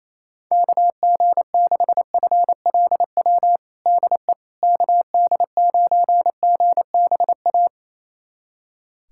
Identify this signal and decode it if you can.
Morse code, heard as a keyed tone, about 28 wpm, 700 Hz. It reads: KG6FLW DE KD9GBA
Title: USA Callsign DE Callsign 28wpm